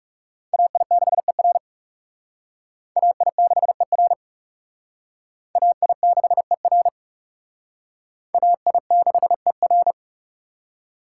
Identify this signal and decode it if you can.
Morse code, heard as a keyed tone, about 37 wpm, 700 Hz. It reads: AI6ER AI6ER AI6ER AI6ER
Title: Speed Racing - USA Call Signs 30wpm